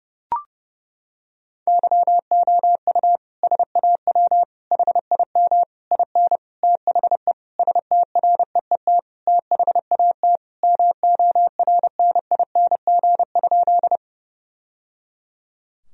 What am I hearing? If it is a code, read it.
Morse code, heard as a keyed tone, about 30 wpm, 700 Hz. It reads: YOU SAW HIM IN THE STREET THAT MORNING?
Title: Sentences from Top 1000 Words - Review Part 3 - 30wpm